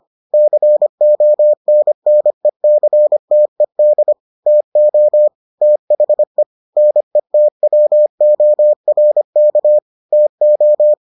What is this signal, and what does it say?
Morse code, heard as a keyed tone, about 25 wpm, 600 Hz. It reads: CONNECTED TO THE NETWORK TO